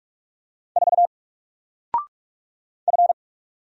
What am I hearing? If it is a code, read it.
Morse code, heard as a keyed tone, about 44 wpm, 700 Hz. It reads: 4 F
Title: Single Letter-Number Alt 45wpm